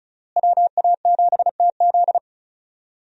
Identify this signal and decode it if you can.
Morse code, heard as a keyed tone, about 36 wpm, 700 Hz. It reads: WA7TZ
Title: USA Call Signs - 5 Characters - Repeated 2x - 35wpm